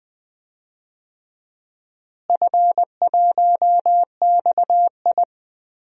Morse code, heard as a keyed tone, about 20 wpm, 700 Hz. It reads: F1XI
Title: DXCC-100 Call Signs - 4 Characters - Repeated 3x - 20wpm